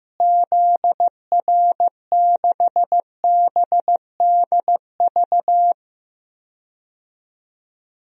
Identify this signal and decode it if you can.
Morse code, heard as a keyed tone, about 15 wpm, 700 Hz. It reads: ZR6BDV
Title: DXCC-100 Call Signs - 6 Characters - Repeated 2x - 15wpm